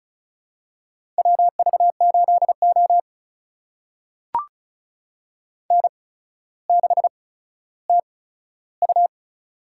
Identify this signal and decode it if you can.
Morse code, heard as a keyed tone, about 36 wpm, spaced as for 15 wpm, 700 Hz. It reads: WV8O N6TU
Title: USA Call Signs - 4 Characters - 8x Character Spacing - 35wpm